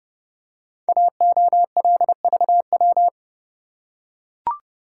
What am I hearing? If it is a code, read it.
Morse code, heard as a keyed tone, about 30 wpm, 700 Hz. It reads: AOLVW
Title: Code Groups - 5 Letters - 30wpm